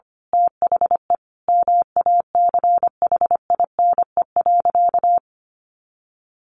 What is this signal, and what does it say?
Morse code, heard as a keyed tone, about 25 wpm, 700 Hz. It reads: THE MACHINE.
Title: A Sound Of Thunder 25wpm